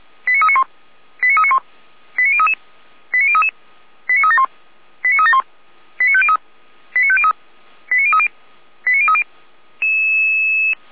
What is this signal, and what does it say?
Five-tone selective call (ZVEI), heard as quick runs of five tones, tones of about 70 ms. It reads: 89391 89391 89033 89033 89371 89371 89503 89503 89030 89030